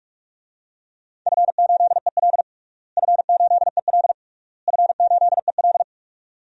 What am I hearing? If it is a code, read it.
Morse code, heard as a keyed tone, about 43 wpm, 690 Hz. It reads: F8EL F8EL F8EL
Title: DXCC-100 Call Signs - 4 Characters - Repeated 3x - 45wpm